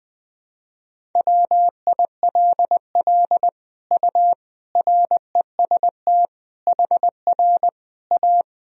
Morse code, heard as a keyed tone, about 20 wpm, 700 Hz. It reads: WILL U REST HR A
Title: Sentences from Top 500 Words and Abbreviations 20wpm